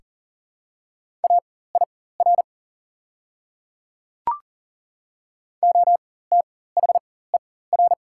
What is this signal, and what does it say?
Morse code, heard as a keyed tone, about 39 wpm, spaced as for 19 wpm, 690 Hz. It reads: AIR OTHER
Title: 4x Character Spacing - Top 300 Words - 40wpm